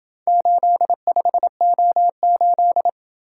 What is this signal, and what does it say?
Morse code, heard as a keyed tone, about 27 wpm, 700 Hz. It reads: 85O8